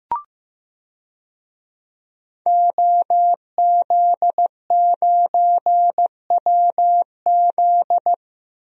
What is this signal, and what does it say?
Morse code, heard as a keyed tone, about 15 wpm, 700 Hz. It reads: OZ9WZ
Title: DXCC-100 Call Signs - 5 Characters - 15wpm